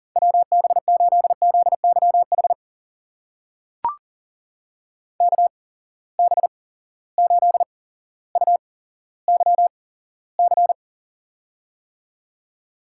Morse code, heard as a keyed tone, about 40 wpm, spaced as for 17 wpm, 700 Hz. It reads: WB8ZYH KB8UYC
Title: USA Call Signs - 6 Characters - 8x Character Spacing - 40wpm